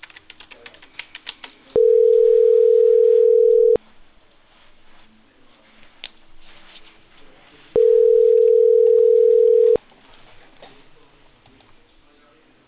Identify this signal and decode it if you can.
Ringback tone (North America), 440 + 480 Hz.